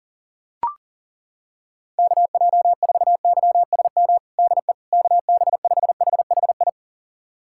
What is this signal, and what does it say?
Morse code, heard as a keyed tone, about 40 wpm, 700 Hz. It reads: KJ4YSM DE KB5HHI
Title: USA Callsign DE Callsign - 6 Characters - 40wpm